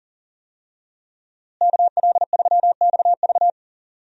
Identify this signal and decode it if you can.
Morse code, heard as a keyed tone, about 39 wpm, 700 Hz. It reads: KL3XV
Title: USA Call Signs - 5 Characters - Repeated 3x - 40wpm